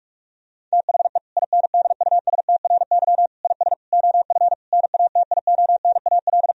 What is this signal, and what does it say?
Morse code, heard as a keyed tone, about 41 wpm, 700 Hz. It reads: THE INDUSTRY IS OF NATIONAL
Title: Sentences from Top 1000 Words - Review Part 4 - 45wpm